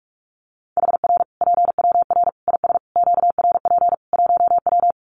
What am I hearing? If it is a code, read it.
Morse code, heard as a keyed tone, about 44 wpm, 700 Hz. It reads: HR PWR IS QRP 1W
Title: Common QSO Phrases 45wpm